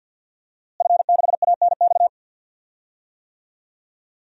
Morse code, heard as a keyed tone, about 52 wpm, 700 Hz. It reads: F6ANX
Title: DXCC-100 Call Signs - 5 Characters - Repeated 2x - 50wpm